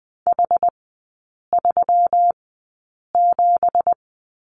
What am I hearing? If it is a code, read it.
Morse code, heard as a keyed tone, about 20 wpm, spaced as for 8 wpm, 700 Hz. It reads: H37
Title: 166 Lesson 27e - Number 7 - Three Characters 20wpm